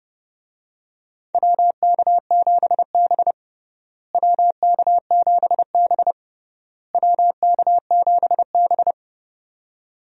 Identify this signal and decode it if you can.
Morse code, heard as a keyed tone, about 30 wpm, 700 Hz. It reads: WK7B WK7B WK7B
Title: USA Call Signs - 4 Characters - Repeated 3x - 30wpm